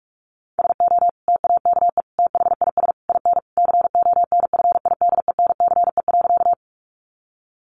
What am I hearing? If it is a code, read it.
Morse code, heard as a keyed tone, about 44 wpm, 700 Hz. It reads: SO TAKE THIS IN CONFIDENCE.
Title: Sentences from Top 1500 Words 45wpm